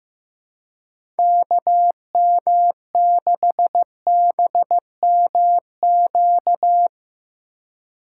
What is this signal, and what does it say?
Morse code, heard as a keyed tone, about 15 wpm, 700 Hz. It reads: KM6BMQ
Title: USA Call Signs - 6 Characters - 3x Character Spacing - 15wpm